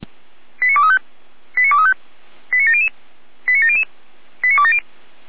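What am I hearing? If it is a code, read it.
Five-tone selective call (ZVEI), heard as quick runs of five tones, tones of about 80 ms. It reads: 89235 89235 89700 89700 89270
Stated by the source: ZVEI five-tone selective call received on an SDR.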